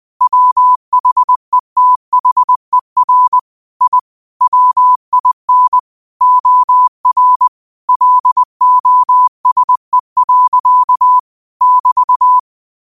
Morse code, heard as a keyed tone, about 20 wpm, 1000 Hz. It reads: WHETHER I WIN OR LOSE. <BT>